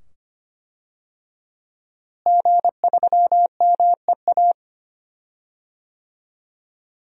Morse code, heard as a keyed tone, about 25 wpm, 700 Hz. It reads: G3MEA
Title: DXCC-100 Call Signs - 5 Characters - Repeated 2x - 25wpm